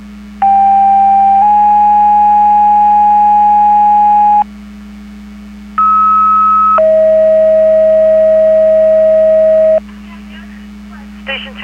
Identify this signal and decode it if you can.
Two-tone page, heard as a one-second tone, then a three-second tone, about 790 then 830 Hz, about 1290 then 640 Hz, with speech. It is heard from 0.4 to 9.8 s.